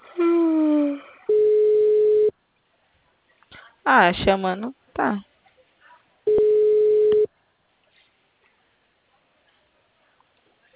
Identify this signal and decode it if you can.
Ringback tone (Europe, 425 Hz), with speech.
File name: out- -2006- - - .5